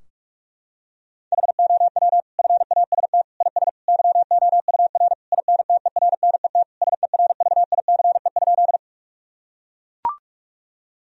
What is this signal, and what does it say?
Morse code, heard as a keyed tone, about 43 wpm, 700 Hz. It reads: HOW FAST IS YOUR INTERNET SERVICE?
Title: Sentences from Top 1000 Words - Review Part 4 - 45wpm